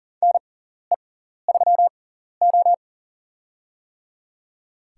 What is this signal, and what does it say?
Morse code, heard as a keyed tone, about 38 wpm, spaced as for 14 wpm, 700 Hz. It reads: NE3O
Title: USA Call Signs - 6x Character Spacing - 40wpm